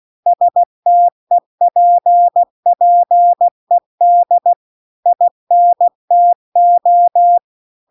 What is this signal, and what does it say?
Morse code, heard as a keyed tone, about 16 wpm, 700 Hz. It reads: STEPPED INTO
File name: CW-The-Big-Twoheaded-River-2-16WPM